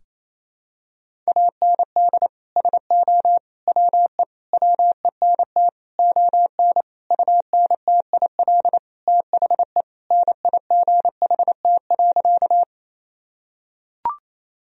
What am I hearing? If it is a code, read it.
Morse code, heard as a keyed tone, about 28 wpm, 700 Hz. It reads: AND SO WE WENT ON UNTIL THE NIGHT.
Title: Sentences from Top 1000 Words - Review Part 1 - 28wpm